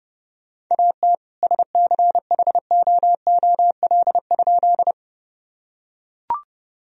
Morse code, heard as a keyed tone, about 30 wpm, 700 Hz. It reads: AT SCHOOL?
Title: Sentences from Top 1000 Words - Review Part 1 - 30wpm